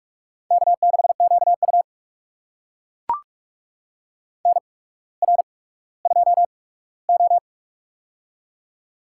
Morse code, heard as a keyed tone, about 44 wpm, spaced as for 19 wpm, 690 Hz. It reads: K6QU NR2O
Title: USA Call Signs - 4 Characters - 8x Character Spacing - 45wpm